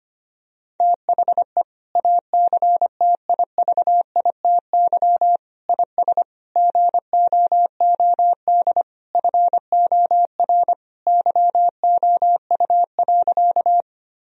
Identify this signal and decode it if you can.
Morse code, heard as a keyed tone, about 25 wpm, 700 Hz. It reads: THE ACTIVITY IS GOOD FOR YOU.
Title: Sentences from Top 1000 Words - Review Part 4 - 25wpm